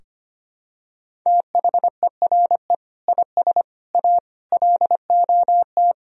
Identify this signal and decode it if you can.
Morse code, heard as a keyed tone, about 25 wpm, 700 Hz. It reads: THERE IS A LOT
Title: Sentences from Top 1000 Words - Review Part 4 - 25wpm